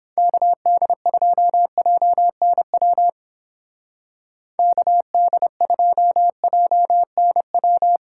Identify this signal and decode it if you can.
Morse code, heard as a keyed tone, about 29 wpm, 700 Hz. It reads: KD2JNW KD2JNW